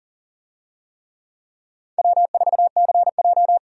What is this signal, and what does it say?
Morse code, heard as a keyed tone, about 40 wpm, 700 Hz. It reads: W4CJ
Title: USA Call Signs - 4 Characters - 40wpm